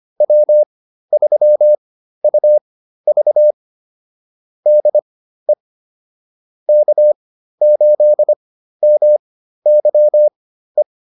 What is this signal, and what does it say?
Morse code, heard as a keyed tone, about 25 wpm, spaced as for 13 wpm, 600 Hz. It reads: W3UV DE K8MYE